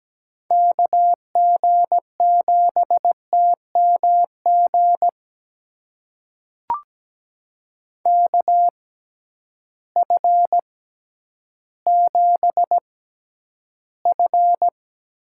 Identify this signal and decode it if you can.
Morse code, heard as a keyed tone, about 17 wpm, spaced as for 10 wpm, 700 Hz. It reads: KG7TMG KF7F
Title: USA Call Signs - 6x Character Spacing - 17wpm